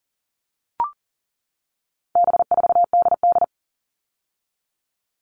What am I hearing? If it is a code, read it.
Morse code, heard as a keyed tone, about 40 wpm, 700 Hz. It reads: B4DD
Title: All Cut Numbers - 4-Digit 40wpm